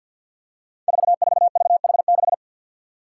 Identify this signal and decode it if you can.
Morse code, heard as a keyed tone, about 53 wpm, 700 Hz. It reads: 44VH6
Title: Five Challenging Letter-Number Combinations 50wpm